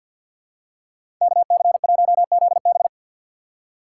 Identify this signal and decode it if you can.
Morse code, heard as a keyed tone, about 50 wpm, 700 Hz. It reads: K C 1 Z B
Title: USA Call Signs - 5 Characters - 6x Character Spacing - 50wpm